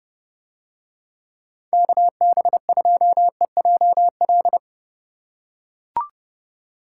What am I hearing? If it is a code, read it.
Morse code, heard as a keyed tone, about 30 wpm, 700 Hz. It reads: KB2EJL